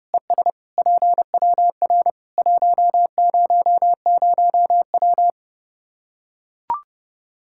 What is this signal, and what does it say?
Morse code, heard as a keyed tone, about 31 wpm, 700 Hz. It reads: ES PWR 100W
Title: Practice QSO - Exchange 2 - Info - 30wpm